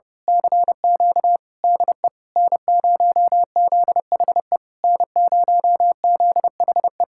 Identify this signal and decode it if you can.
Morse code, heard as a keyed tone, about 28 wpm, 700 Hz. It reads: CQ DE N0ZHE N0ZHE
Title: Speed-Racing Common QSO Phrases Encore - 30wpm